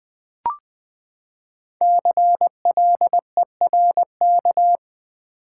Morse code, heard as a keyed tone, about 20 wpm, 700 Hz. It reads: CLERK
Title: Five Letter Words - 20wpm